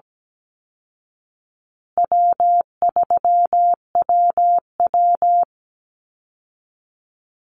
Morse code, heard as a keyed tone, about 17 wpm, 700 Hz. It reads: W3WW
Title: Call Signs Repeated 3x - 17wpm